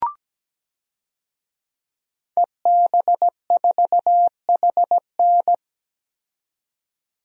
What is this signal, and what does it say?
Morse code, heard as a keyed tone, about 17 wpm, 700 Hz. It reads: EB4HN
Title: DXCC-100 Call Signs - 5 Characters - 17wpm